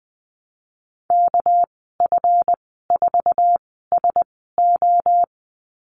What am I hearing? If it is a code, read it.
Morse code, heard as a keyed tone, about 20 wpm, spaced as for 14 wpm, 700 Hz. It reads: KF4SO
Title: 213 Lesson 34g - Number 8 - Call Signs 20wpm